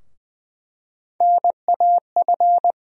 Morse code, heard as a keyed tone, about 20 wpm, 700 Hz. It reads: NAF
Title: USA Call Sign Postfixes - 3 Letters 20wpm